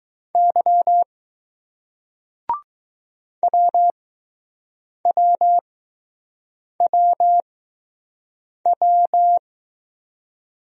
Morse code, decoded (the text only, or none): Y WWWW